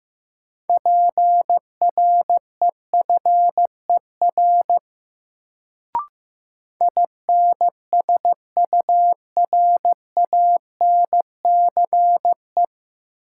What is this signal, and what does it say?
Morse code, heard as a keyed tone, about 15 wpm, 700 Hz. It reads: PREFER INSURANCE
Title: Sentences from Top 1100 Words - New Word Drill - 15wpm